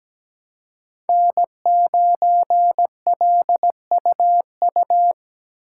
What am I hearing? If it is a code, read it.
Morse code, heard as a keyed tone, about 17 wpm, 700 Hz. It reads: N9LUU
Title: USA Call Signs - 5 Characters - 4x Character Spacing - 17wpm